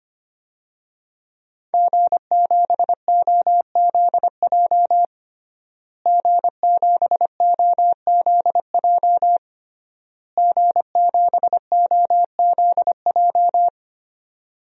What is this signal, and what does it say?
Morse code, heard as a keyed tone, about 25 wpm, 700 Hz. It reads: G7OZJ G7OZJ G7OZJ
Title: DXCC-100 Call Signs - 5 Characters - Repeated 3x - 25wpm